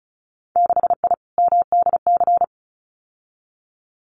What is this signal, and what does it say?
Morse code, heard as a keyed tone, about 34 wpm, 700 Hz. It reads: 6I MDC
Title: Winter Field Day Exchanges Encore 35wpm